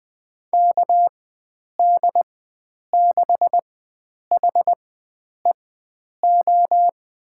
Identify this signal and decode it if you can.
Morse code, heard as a keyed tone, about 20 wpm, spaced as for 9 wpm, 700 Hz. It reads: KD6HEO
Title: USA Call Signs - 4x Character Spacing - 20wpm